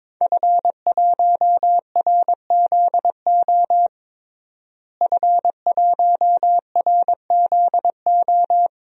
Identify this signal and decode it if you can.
Morse code, heard as a keyed tone, about 22 wpm, 700 Hz. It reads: F1RZO F1RZO
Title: DXCC-100 Call Signs - 5 Characters - Repeated 2x - 22wpm